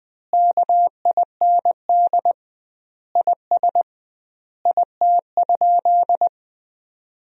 Morse code, heard as a keyed tone, about 20 wpm, 700 Hz. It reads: KIND IS IT?
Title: 2x Word-Spacing - Sentences from Top 200 Words - 20wpm